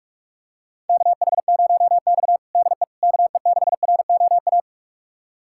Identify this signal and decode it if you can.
Morse code, heard as a keyed tone, about 45 wpm, 700 Hz. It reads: KH0X DE KE6ROA
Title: USA Callsign DE Callsign 45wpm